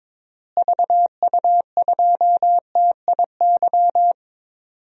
Morse code, heard as a keyed tone, about 22 wpm, 700 Hz. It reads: VU2TIY
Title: DXCC-100 Call Signs - 6 Characters - Repeated 3x - 22wpm